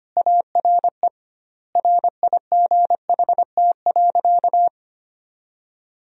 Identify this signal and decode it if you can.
Morse code, heard as a keyed tone, about 25 wpm, 700 Hz. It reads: ARE RIGHT.
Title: 2x Word-Spacing - Sentences from Top 500 Words - 25wpm